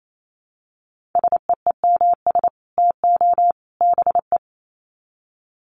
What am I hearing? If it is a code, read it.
Morse code, heard as a keyed tone, about 28 wpm, 700 Hz. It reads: SEEMS TO BE
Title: Sets of 3 Words 28wpm